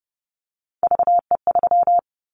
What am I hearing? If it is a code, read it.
Morse code, heard as a keyed tone, about 30 wpm, 700 Hz. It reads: VE3
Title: Callsign Prefixes for Top 100 DXCC - 30wpm